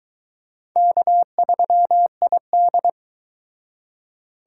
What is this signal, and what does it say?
Morse code, heard as a keyed tone, about 22 wpm, 700 Hz. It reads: K3ID